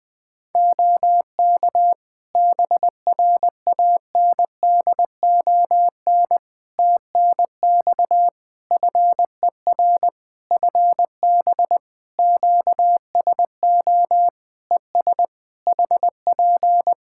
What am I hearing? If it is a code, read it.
Morse code, heard as a keyed tone, about 20 wpm, 700 Hz. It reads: OK BRANDON TNX FER FB QSO ES HP
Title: K1USN SST QSO Practice 20wpm